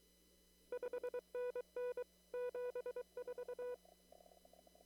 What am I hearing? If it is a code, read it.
Morse code, heard as a keyed tone, about 24 wpm, 500 Hz. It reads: 5NN 74
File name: CW / kt5j wpx cw-op-k5pi